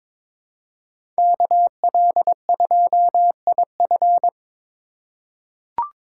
Morse code, heard as a keyed tone, about 22 wpm, 700 Hz. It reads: KL2IF
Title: USA Call Signs - 5 Characters - 2x Character Spacing - 22wpm